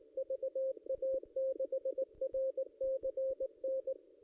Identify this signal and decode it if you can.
Morse code, heard as a keyed tone, about 19 wpm, 530 Hz, with radio noise. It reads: VA6RCN
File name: contests / ss-cw-k5tr-op-n5rz -1600